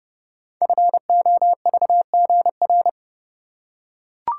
Morse code, heard as a keyed tone, about 29 wpm, 700 Hz. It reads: FOVGR
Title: Code Groups - 5 Letters - 30wpm